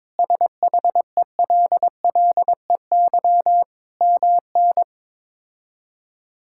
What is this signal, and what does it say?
Morse code, heard as a keyed tone, about 22 wpm, 700 Hz. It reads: SHELLEY MN
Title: K1USN SST QSO Practice - 22wpm